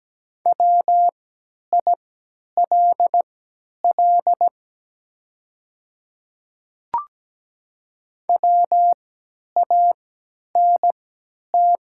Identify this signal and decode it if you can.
Morse code, heard as a keyed tone, about 17 wpm, spaced as for 10 wpm, 700 Hz. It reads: WILL WANT
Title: 3x Character Spacing - Top 300 Words - 17wpm